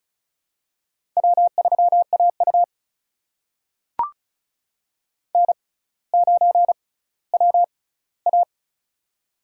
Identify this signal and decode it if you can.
Morse code, heard as a keyed tone, about 36 wpm, spaced as for 18 wpm, 700 Hz. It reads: W3AU N9WA
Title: USA Call Signs - 4 Characters - 6x Character Spacing - 35wpm